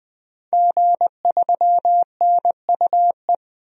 Morse code, heard as a keyed tone, about 20 wpm, 700 Hz. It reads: G3NUE
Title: DXCC-100 Call Signs - 5 Characters - 20wpm